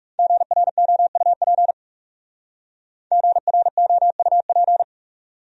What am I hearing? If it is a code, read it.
Morse code, heard as a keyed tone, about 37 wpm, 700 Hz. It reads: GROUP GROUP
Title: Speed-Racing Top 300 Words 30wpm